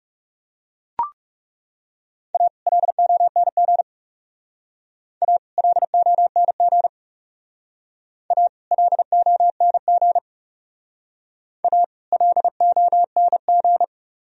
Morse code, heard as a keyed tone, about 37 wpm, 700 Hz. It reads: A LONG A LONG A LONG A LONG
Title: Speed-Racing - Sets of 2 Words 30wpm